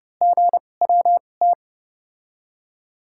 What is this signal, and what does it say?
Morse code, heard as a keyed tone, about 30 wpm, spaced as for 22 wpm, 700 Hz. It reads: GWT
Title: USA Call Signs - 5 Characters - 2x Character Spacing - 30wpm